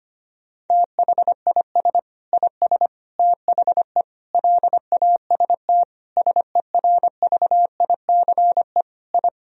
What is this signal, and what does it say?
Morse code, heard as a keyed tone, about 25 wpm, 700 Hz. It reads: THIS IS THE LAST SERVICE I
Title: Sentences from Top 1000 Words - Review Part 4 - 25wpm